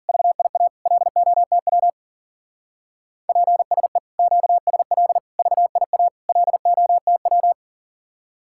Morse code, heard as a keyed tone, about 51 wpm, 700 Hz. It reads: VIA LOTW PSE QSL VIA LOTW